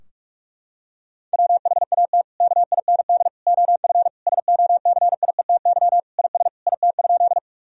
Morse code, heard as a keyed tone, about 46 wpm, 700 Hz. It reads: WHAT KIND OF SOCIETY IS IT?
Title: Sentences from Top 1000 Words - Review Part 4 - 45wpm